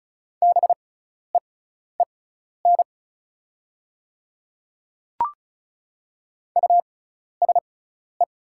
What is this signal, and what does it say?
Morse code, heard as a keyed tone, about 36 wpm, spaced as for 12 wpm, 700 Hz. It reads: BEEN USE
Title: 6x Character Spacing - Top 100 Words - 35wpm